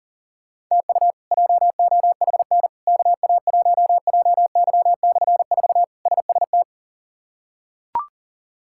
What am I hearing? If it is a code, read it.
Morse code, heard as a keyed tone, about 40 wpm, 700 Hz. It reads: TU JOHN KA1JY/4 SST
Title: K1USN SST QSO Practice - 40wpm